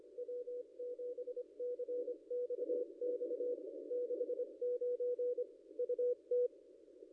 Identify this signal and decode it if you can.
Morse code, heard as a keyed tone, about 25 wpm, 530 Hz. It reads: W7CXX 69 UT
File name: contests / ss-cw-k5tr-op-n5rz -0800